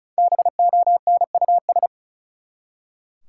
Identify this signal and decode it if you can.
Morse code, heard as a keyed tone, about 34 wpm, 700 Hz. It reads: BONUS
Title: Five Letter Words - 35wpm